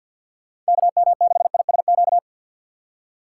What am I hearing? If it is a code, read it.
Morse code, heard as a keyed tone, about 65 wpm, spaced as for 57 wpm, 700 Hz. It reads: KM6ISQ
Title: Speed Racing - USA 2x3 Call Signs 50wpm